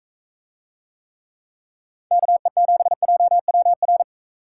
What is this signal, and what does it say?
Morse code, heard as a keyed tone, about 42 wpm, 700 Hz. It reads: KE7JWR